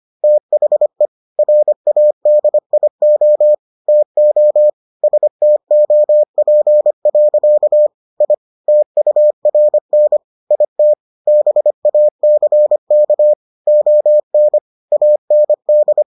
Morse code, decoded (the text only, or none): THE RADIO TO STOP. I TURN IT BACK ON AND